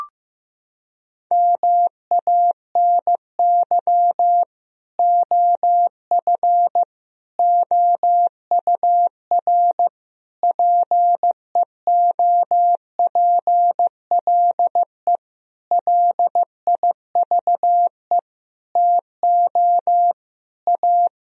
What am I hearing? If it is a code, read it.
Morse code, heard as a keyed tone, about 15 wpm, 700 Hz. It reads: MANY OF OUR PEOPLE LIVE TO A
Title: Sentences from Top 1000 Words - Review Part 3 - 15wpm